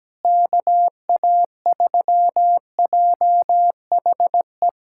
Morse code, heard as a keyed tone, about 17 wpm, 700 Hz. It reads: KA3JHE
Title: USA Callsign DE Callsign - 6 Characters - 17wpm